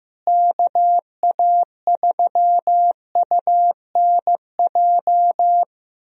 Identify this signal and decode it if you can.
Morse code, heard as a keyed tone, about 16 wpm, 700 Hz. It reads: KA3UNJ
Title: USA Call Signs - 6 Characters - Repeated 3x - 15wpm